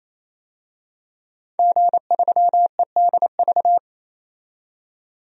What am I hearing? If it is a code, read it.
Morse code, heard as a keyed tone, about 28 wpm, 700 Hz. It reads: G3EDV
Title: DXCC-100 Call Signs - 5 Characters - 28wpm